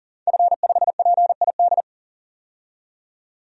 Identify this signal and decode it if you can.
Morse code, heard as a keyed tone, about 41 wpm, 700 Hz. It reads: F5PID
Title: DXCC-100 Call Signs - 5 Characters - 40wpm